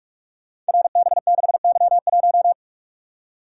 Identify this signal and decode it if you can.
Morse code, heard as a keyed tone, about 44 wpm, 700 Hz. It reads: RB6Y1
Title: Code Groups - 5 Letter:Numbers - 45wpm